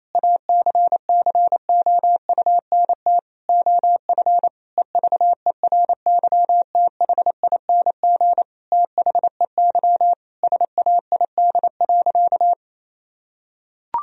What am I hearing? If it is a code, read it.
Morse code, heard as a keyed tone, about 28 wpm, 700 Hz. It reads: ACCOUNT OF EVERYTHING THEY SAID.
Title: Sentences from Top 1000 Words - Review Part 3 - 28wpm